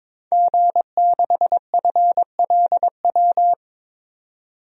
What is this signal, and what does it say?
Morse code, heard as a keyed tone, about 22 wpm, 700 Hz. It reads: G6FLW
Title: DXCC-100 Call Signs - 5 Characters - 22wpm